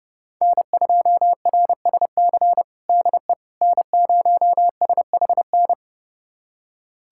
Morse code, heard as a keyed tone, about 30 wpm, 700 Hz. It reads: N2RSC DE N0SHN
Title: USA Callsign DE Callsign - 5 Characters - 30wpm